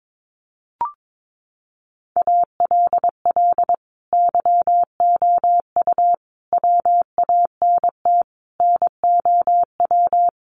Morse code, heard as a keyed tone, about 22 wpm, 700 Hz. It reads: ALL YOU WANT NOW
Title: Sentences from Top 1200 Words 22wpm